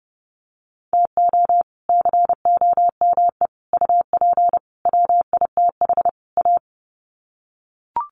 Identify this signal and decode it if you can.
Morse code, heard as a keyed tone, about 30 wpm, 700 Hz. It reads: TO COME UP WITH A
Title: Sets of 5 Words 30wpm